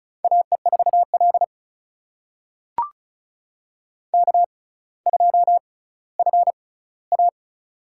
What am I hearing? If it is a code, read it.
Morse code, heard as a keyed tone, about 36 wpm, spaced as for 18 wpm, 700 Hz. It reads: AE4L K2FA
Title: USA Call Signs - 6x Character Spacing - 35wpm